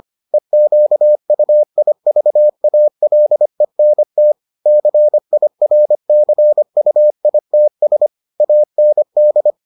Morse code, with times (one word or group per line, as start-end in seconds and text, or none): 0.3-4.3 EQUIVALENT
4.7-8.1 CIRCUITS
8.4-9.5 AND